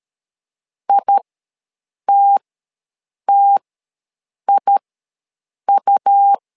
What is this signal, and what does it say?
Morse code, heard as a keyed tone, about 13 wpm, spaced as for 7 wpm, 800 Hz. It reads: ITTIU
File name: LEZIONI / 2 UHDN / 2 U ws 12wpm 1655